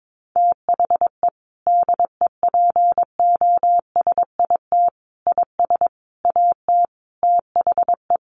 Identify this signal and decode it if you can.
Morse code, heard as a keyed tone, about 22 wpm, 700 Hz. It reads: THE DEPOSIT IS AT THE
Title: Sentences from Top 1300 Words 22wpm